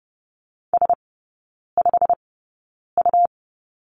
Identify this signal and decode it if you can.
Morse code, heard as a keyed tone, about 30 wpm, spaced as for 9 wpm, 700 Hz. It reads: S5U
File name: Lesson 17e - Letter u - Three Characters 30wpm